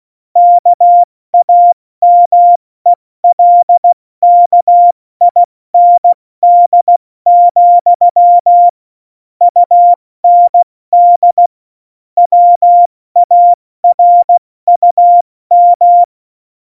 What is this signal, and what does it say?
Morse code, heard as a keyed tone, about 16 wpm, spaced as for 14 wpm, 700 Hz. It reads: KAMELKIND, UND WARUM
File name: Morsekurs / Witze / Witz 06. November 2018, 70 F80